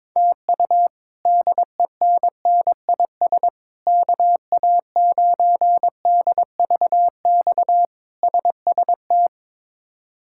Morse code, heard as a keyed tone, about 22 wpm, 700 Hz. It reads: TU DENNIS KA9DVX SST
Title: K1USN SST QSO Practice - 22wpm